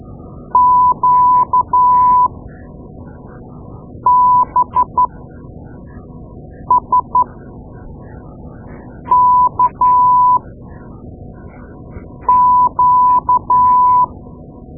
Morse code, heard as a keyed tone, about 12 wpm, spaced as for 4 wpm, 1000 Hz. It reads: QBSKQ
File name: Morse Lessons / Morse Lessons / Tape-2A